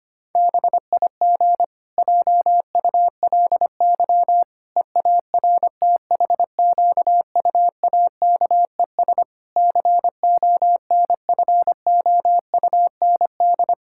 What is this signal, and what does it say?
Morse code, heard as a keyed tone, about 25 wpm, 700 Hz. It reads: BIG JULY EARTHQUAKES CONFOUND
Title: Pangram Challenge v2 - 25wpm